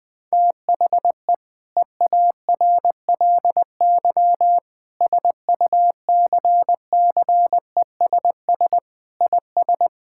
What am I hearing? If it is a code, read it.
Morse code, heard as a keyed tone, about 20 wpm, 700 Hz. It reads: THE EARLY SUCCESS IS
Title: Sentences with 6-Word Phrases - Spoken Beginning - 20wpm